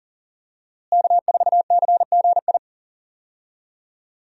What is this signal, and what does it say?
Morse code, heard as a keyed tone, about 37 wpm, 690 Hz. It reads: K4CGI
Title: USA Call Signs - 5 Characters - Repeated 3x - 40wpm